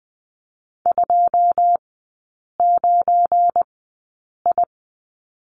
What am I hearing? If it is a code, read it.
Morse code, heard as a keyed tone, about 20 wpm, spaced as for 8 wpm, 700 Hz. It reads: 29I
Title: 172 Lesson 28e - Number 9 - Three Characters 20wpm